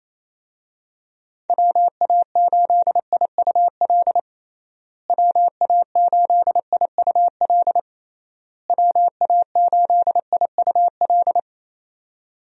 Morse code, heard as a keyed tone, about 28 wpm, 700 Hz. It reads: WA8IUL WA8IUL WA8IUL
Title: USA Call Signs - 6 Characters - Repeated 3x - 28wpm